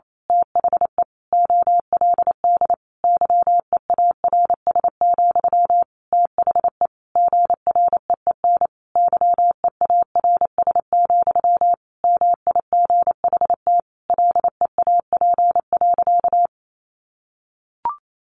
Morse code, heard as a keyed tone, about 28 wpm, 700 Hz. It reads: THE OLD YEARS, THE GREEN YEARS, MIGHT LEAP.
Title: A Sound Of Thunder 28wpm